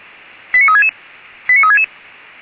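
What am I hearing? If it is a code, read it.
Five-tone selective call (ZVEI), heard as quick runs of five tones, tones of about 80 ms. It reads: 89366 89366
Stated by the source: ZVEI five-tone call received by SDR, from the Feuerwehr Orlamünde fire service.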